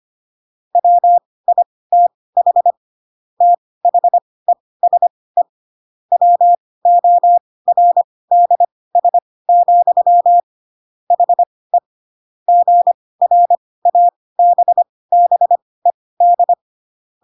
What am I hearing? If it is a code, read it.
Morse code, heard as a keyed tone, about 25 wpm, spaced as for 18 wpm, 700 Hz. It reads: WITH THESE WORDS, HE GRABBED
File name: Pinocchio / Pinocchio ch1 12 25 ik1hge